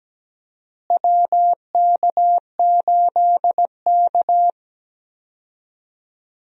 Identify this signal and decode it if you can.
Morse code, heard as a keyed tone, about 17 wpm, 700 Hz. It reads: WK8K
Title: USA Call Signs - 4 Characters - Repeated 2x - 17wpm